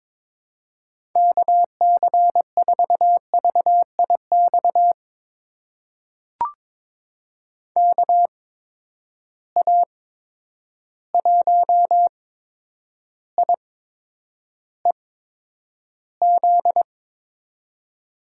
Morse code, heard as a keyed tone, about 22 wpm, spaced as for 9 wpm, 700 Hz. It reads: KC4VIX KA1IEZ
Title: USA Call Signs - 6 Characters - 8x Character Spacing - 22wpm